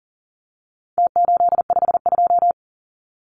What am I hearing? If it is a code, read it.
Morse code, heard as a keyed tone, about 40 wpm, 700 Hz. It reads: T852
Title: Numbers - Contest Style - 4-Digit 40wpm